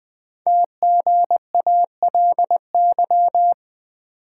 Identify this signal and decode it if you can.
Morse code, heard as a keyed tone, about 20 wpm, 700 Hz. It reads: TGALY
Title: Code Groups - 5 Letters - 20wpm